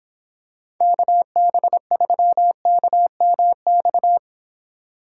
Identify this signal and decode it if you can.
Morse code, heard as a keyed tone, about 26 wpm, 700 Hz. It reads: KB3KMX